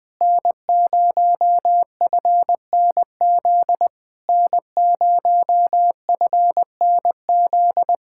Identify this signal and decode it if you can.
Morse code, heard as a keyed tone, about 21 wpm, 700 Hz. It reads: N0FNZ N0FNZ
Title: Speed-Racing Common QSO Phrases - 20wpm